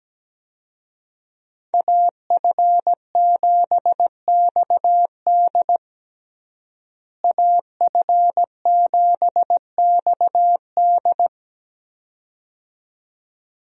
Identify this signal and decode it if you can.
Morse code, heard as a keyed tone, about 17 wpm, 700 Hz. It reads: AF7XD AF7XD
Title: USA Call Signs - 5 Characters - Repeated 2x - 17wpm